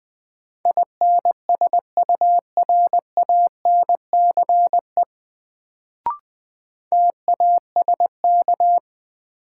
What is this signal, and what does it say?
Morse code, heard as a keyed tone, about 20 wpm, 700 Hz. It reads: INSURANCE TASK
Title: Sentences from Top 1100 Words - New Word Drill Alt - 20wpm